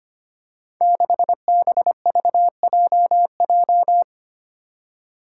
Morse code, heard as a keyed tone, about 25 wpm, 700 Hz. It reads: 6BVJJ
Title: Five Challenging Letter-Number Combinations 25wpm